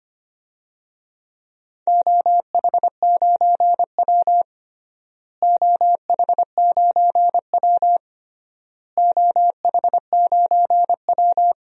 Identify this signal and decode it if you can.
Morse code, heard as a keyed tone, about 25 wpm, 700 Hz. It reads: OH9W OH9W OH9W
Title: DXCC-100 Call Signs - 4 Characters - Repeated 3x - 25wpm